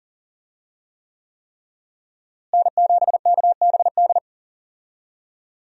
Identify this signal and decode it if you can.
Morse code, heard as a keyed tone, about 40 wpm, 700 Hz. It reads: N7KBD
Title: USA Call Signs - 5 Characters - Repeated 3x - 40wpm